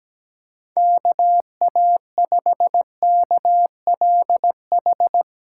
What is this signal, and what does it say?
Morse code, heard as a keyed tone, about 17 wpm, 700 Hz. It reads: KA5KLH
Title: USA Call Signs - 4x Character Spacing - 17wpm